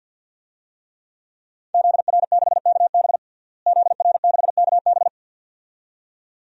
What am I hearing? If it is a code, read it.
Morse code, heard as a keyed tone, about 49 wpm, 700 Hz. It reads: ZR6KB ZR6KB
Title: DXCC-100 Call Signs - 5 Characters - Repeated 2x - 50wpm